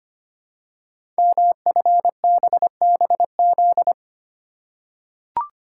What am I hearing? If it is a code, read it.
Morse code, heard as a keyed tone, about 25 wpm, 700 Hz. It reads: MFBBZ
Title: Code Groups - 5 Letters - 25wpm